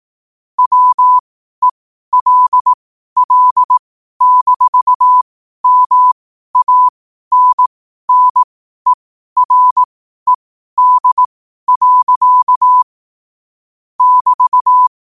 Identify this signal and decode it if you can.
Morse code, heard as a keyed tone, about 18 wpm, spaced as for 12 wpm, 1000 Hz. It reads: WELL-MANNERED. <BT>
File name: qotdfiles / 12wpm